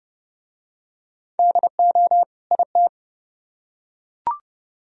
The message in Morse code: DO IT

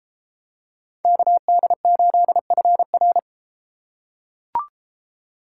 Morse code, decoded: KD8FR

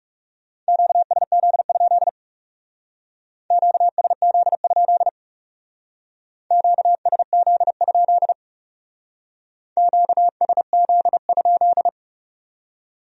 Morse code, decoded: QSZ? QSZ? QSZ? QSZ?

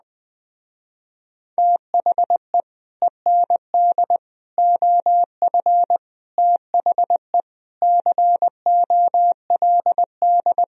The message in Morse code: THE END OF THE COLD

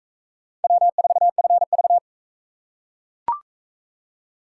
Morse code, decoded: W4FV

